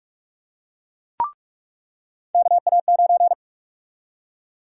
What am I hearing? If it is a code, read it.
Morse code, heard as a keyed tone, about 46 wpm, 700 Hz. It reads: KA9
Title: USA Call Sign Prefixes - 45wpm